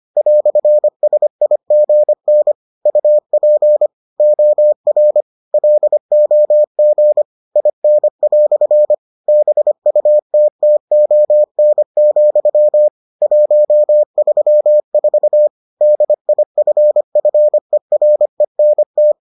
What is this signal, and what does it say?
Morse code, heard as a keyed tone, about 25 wpm, 600 Hz. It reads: "SIGN UP OR LOG IN" BUTTON, 134 DIFFERENT